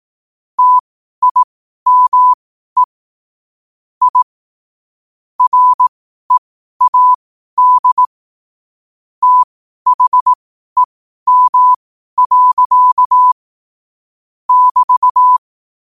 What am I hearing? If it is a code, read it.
Morse code, heard as a keyed tone, about 18 wpm, spaced as for 12 wpm, 1000 Hz. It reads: TIME I READ THEM. <BT>
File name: qotdfiles / 12wpm